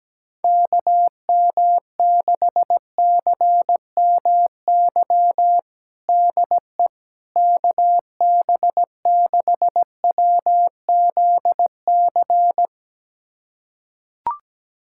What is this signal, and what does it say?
Morse code, heard as a keyed tone, about 17 wpm, 700 Hz. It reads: KM6CMY DE KB6WZC
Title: USA Callsign DE Callsign - 6 Characters - 17wpm